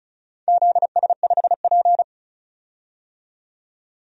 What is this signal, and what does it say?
Morse code, heard as a keyed tone, about 36 wpm, 700 Hz. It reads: ZS5P
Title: DXCC-100 Call Signs - 4 Characters - 35wpm